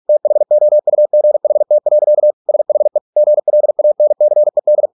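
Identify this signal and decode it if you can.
Morse code, heard as a keyed tone, about 48 wpm, 600 Hz. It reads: THOUGHT. SHE GLANCED
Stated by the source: Morse practice at 45 wpm, the novel Never by Ken Follett in English, from CW med Gnister.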